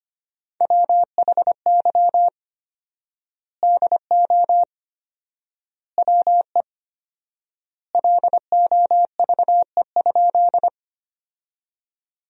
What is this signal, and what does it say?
Morse code, heard as a keyed tone, about 25 wpm, 700 Hz. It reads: WHY DO WE LOVE?
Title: 4x Word-Spacing - Sentences from Top 400 Words - 25wpm